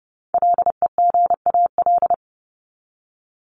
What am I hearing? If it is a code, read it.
Morse code, heard as a keyed tone, about 30 wpm, 700 Hz. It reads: LEGAL
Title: Top 1001-1250 Words 30wpm